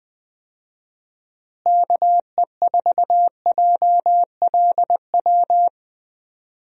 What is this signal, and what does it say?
Morse code, heard as a keyed tone, about 20 wpm, 700 Hz. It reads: KE4JLW
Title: USA Call Signs - 6 Characters - Repeated 2x - 20wpm